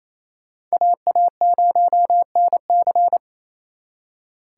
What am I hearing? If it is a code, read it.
Morse code, heard as a keyed tone, about 35 wpm, 700 Hz. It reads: AA0NC